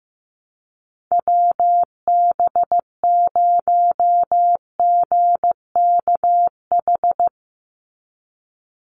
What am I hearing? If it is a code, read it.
Morse code, heard as a keyed tone, about 15 wpm, 700 Hz. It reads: WB0GKH
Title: Call Signs Repeated 3x - 15wpm